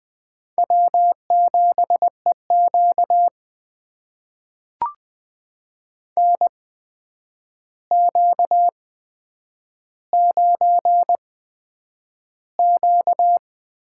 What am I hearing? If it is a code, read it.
Morse code, heard as a keyed tone, about 20 wpm, spaced as for 9 wpm, 700 Hz. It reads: W7EQ NQ9Q
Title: USA Call Signs - 4 Characters - 8x Character Spacing - 20wpm